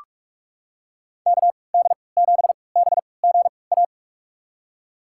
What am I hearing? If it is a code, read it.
Morse code, heard as a keyed tone, about 44 wpm, spaced as for 25 wpm, 700 Hz. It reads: KD7BGA
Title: USA Call Signs - 6 Characters - 3x Character Spacing - 45wpm